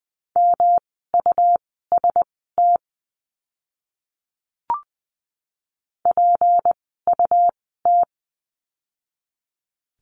Morse code, heard as a keyed tone, about 20 wpm, spaced as for 15 wpm, 700 Hz. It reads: MUST PUT
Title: 237 Lesson 38f - Letter q - Words 20wpm